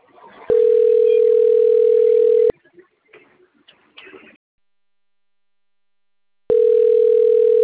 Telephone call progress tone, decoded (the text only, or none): ringback tone